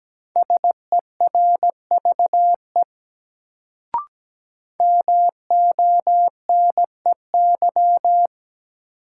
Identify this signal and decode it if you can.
Morse code, heard as a keyed tone, about 17 wpm, 700 Hz. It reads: SERVE MONEY
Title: Sentences from Top 400 Words - New Word Drill - 17wpm